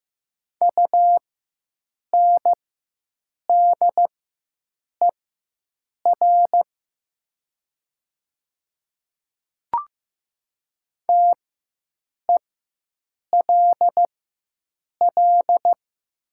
Morse code, heard as a keyed tone, about 15 wpm, spaced as for 7 wpm, 690 Hz. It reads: UNDER TELL
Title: 4x Character Spacing - Top 200 Words - 15wpm